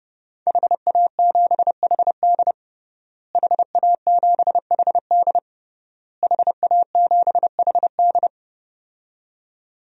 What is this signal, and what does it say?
Morse code, heard as a keyed tone, about 32 wpm, 700 Hz. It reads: HA7HD HA7HD HA7HD
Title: DXCC-100 Call Signs - 5 Characters - Repeated 3x - 30wpm